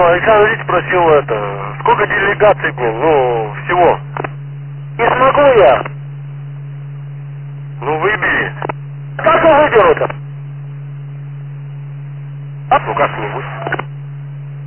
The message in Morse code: none

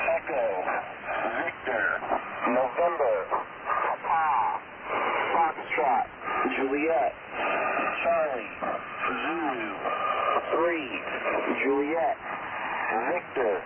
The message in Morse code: none

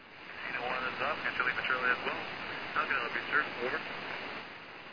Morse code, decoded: none

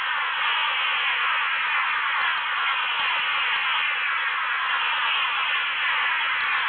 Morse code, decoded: none

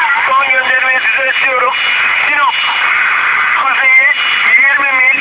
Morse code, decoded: none

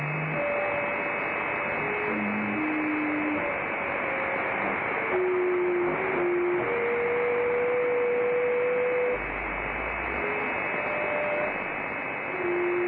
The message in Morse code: none

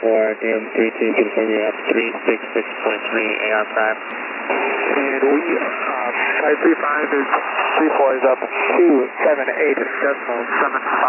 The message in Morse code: none

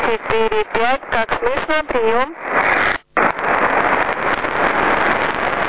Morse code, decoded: none

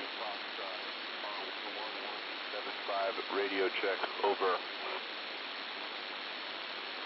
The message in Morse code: none

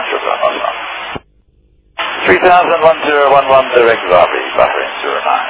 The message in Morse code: none

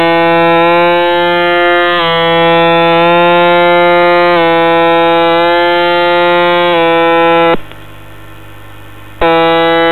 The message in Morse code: none

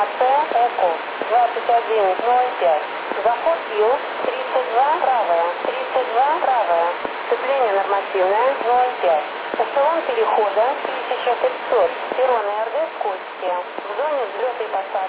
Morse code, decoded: none

